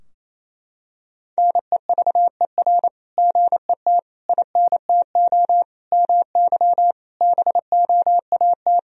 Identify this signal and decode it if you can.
Morse code, heard as a keyed tone, about 28 wpm, 700 Hz. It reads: NEVER GET INTO MY BOAT
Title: Sentences from Top 1000 Words - Review Part 3 - 28wpm